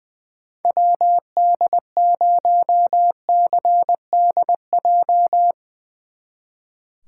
Morse code, heard as a keyed tone, about 20 wpm, 700 Hz. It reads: WD0CDJ
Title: Speed Racing - USA Call Signs 20wpm